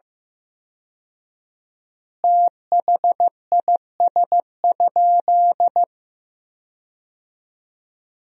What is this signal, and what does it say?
Morse code, heard as a keyed tone, about 16 wpm, 700 Hz. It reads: THIS?
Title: 4x Word-Spacing - Sentences from Top 200 Words - 15wpm